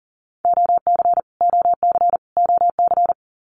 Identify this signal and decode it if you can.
Morse code, heard as a keyed tone, about 40 wpm, 700 Hz. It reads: OC OC OC
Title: Abbreviations Repeated 3x - 40wpm